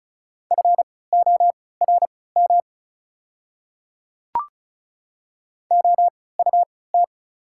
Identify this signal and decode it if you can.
Morse code, heard as a keyed tone, about 34 wpm, spaced as for 20 wpm, 690 Hz. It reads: FORM OUT
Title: 3x Character Spacing - Top 300 Words - 35wpm